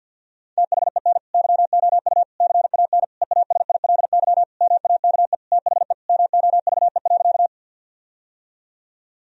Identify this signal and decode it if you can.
Morse code, heard as a keyed tone, about 47 wpm, 700 Hz. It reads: THEN YOU CAN EASILY MAKE THE MOVE.
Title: Sentences from Top 1000 Words - Review Part 1 - 50wpm